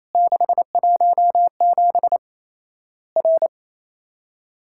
Morse code, heard as a keyed tone, about 28 wpm, 700 Hz. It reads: 617 R
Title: Worked All Europe DX Contest - QTC Exchange - 28wpm